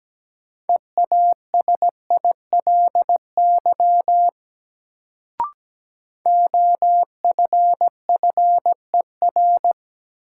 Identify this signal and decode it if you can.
Morse code, heard as a keyed tone, about 17 wpm, 700 Hz. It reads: EASILY OFFER
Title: Sentences from Top 900 Words - New Word Drill - 17wpm